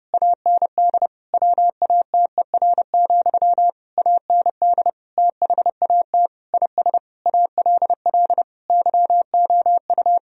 Morse code, heard as a keyed tone, about 30 wpm, 700 Hz. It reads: AND WATER, AND THAT IS ALL YOU
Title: Sentences from Top 1000 Words - Review Part 2 - 30wpm